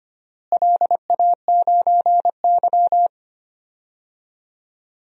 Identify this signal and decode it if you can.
Morse code, heard as a keyed tone, about 25 wpm, 700 Hz. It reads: LA9Y
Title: DXCC-100 Call Signs - 4 Characters - Repeated 3x - 25wpm